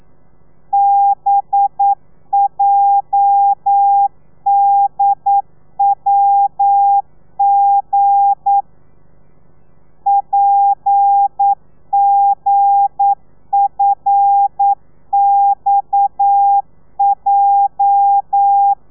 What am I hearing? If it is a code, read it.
Morse code, heard as a keyed tone, about 9 wpm, 800 Hz. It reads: BJDWG PGFXJ
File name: telegrafia / i2tfj / Lez8 40